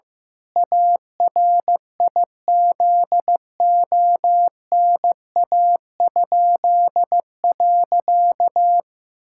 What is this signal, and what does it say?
Morse code, heard as a keyed tone, about 14 wpm, 700 Hz. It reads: ARIZONA?.
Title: Pangram Challenge v2 - 15wpm